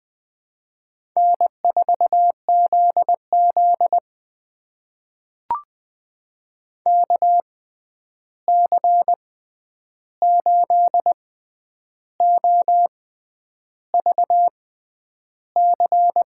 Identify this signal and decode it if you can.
Morse code, heard as a keyed tone, about 20 wpm, spaced as for 9 wpm, 700 Hz. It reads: N4ZZ KC8OVC
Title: USA Call Signs - 6x Character Spacing - 20wpm